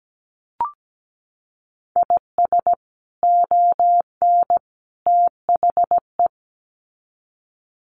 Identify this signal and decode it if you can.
Morse code, heard as a keyed tone, about 17 wpm, 700 Hz. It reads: IS ON THE
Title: Sets of 3 Words 17wpm